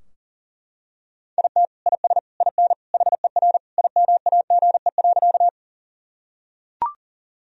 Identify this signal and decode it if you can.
Morse code, heard as a keyed tone, about 37 wpm, 700 Hz. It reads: IT IS IN HER IMAGE.
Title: Sentences from Top 1000 Words - Review Part 4 - 40wpm